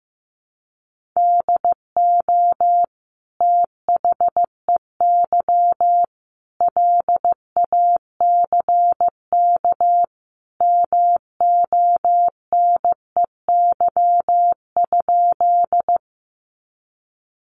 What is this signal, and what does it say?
Morse code, heard as a keyed tone, about 15 wpm, 700 Hz. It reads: DO THEY LACK MONEY?
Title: Sentences from Top 1100 Words 15wpm